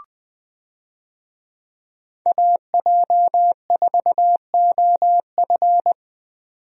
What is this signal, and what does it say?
Morse code, heard as a keyed tone, about 20 wpm, 700 Hz. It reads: AJ4OF
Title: USA Call Signs - 5 Characters - 20wpm